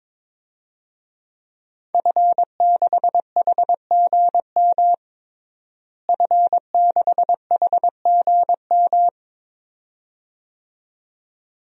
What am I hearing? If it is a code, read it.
Morse code, heard as a keyed tone, about 22 wpm, 700 Hz. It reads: F6HGM F6HGM
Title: DXCC-100 Call Signs - 5 Characters - Repeated 2x - 22wpm